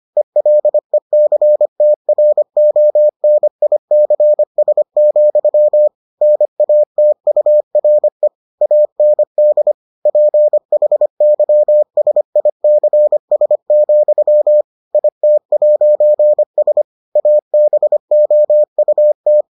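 Morse code, heard as a keyed tone, about 25 wpm, 600 Hz. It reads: ELECTRONICS, NATURE AND PHYSICS, IT'S ABOUT
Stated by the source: Morse edition of the Foundations of Amateur Radio episodes.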